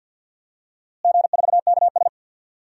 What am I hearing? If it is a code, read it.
Morse code, heard as a keyed tone, about 47 wpm, 700 Hz. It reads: G4KS
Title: DXCC-100 Call Signs - 4 Characters - Repeated 2x - 50wpm